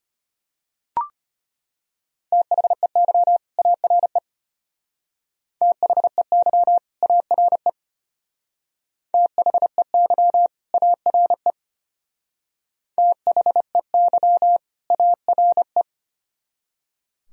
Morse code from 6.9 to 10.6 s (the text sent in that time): ARE THEY